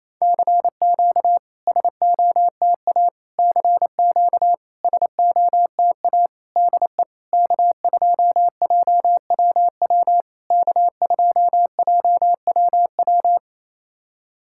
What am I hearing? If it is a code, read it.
Morse code, heard as a keyed tone, about 28 wpm, 700 Hz. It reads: CQ SOTA CQ SOTA DE K2JWW K2JWW
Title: SOTA QSO Practice 28wpm